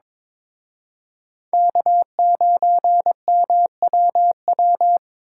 Morse code, heard as a keyed tone, about 25 wpm, 700 Hz. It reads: K9MWW
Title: Speed Racing - USA 1x3 Call Signs 22wpm